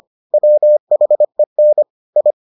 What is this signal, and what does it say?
Morse code, heard as a keyed tone, about 25 wpm, 600 Hz. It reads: WHEN I